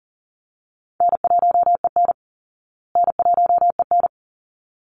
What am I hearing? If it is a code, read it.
Morse code, heard as a keyed tone, about 40 wpm, 700 Hz. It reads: N1EN N1EN
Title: Easy Call Signs Repeated 2x - 40wpm